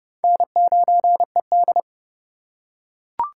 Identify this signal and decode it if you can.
Morse code, heard as a keyed tone, about 30 wpm, 700 Hz. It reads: N9ED
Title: USA Call Signs - 4 Characters - 2x Character Spacing - 30wpm